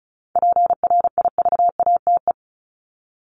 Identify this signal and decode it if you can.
Morse code, heard as a keyed tone, about 34 wpm, 700 Hz. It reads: PRIVATE
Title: Top 751-1000 Words 35wpm